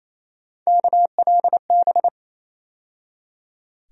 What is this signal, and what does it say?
Morse code, heard as a keyed tone, about 28 wpm, 700 Hz. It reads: KLB
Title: USA Call Sign Postfixes - 3 Letters 28wpm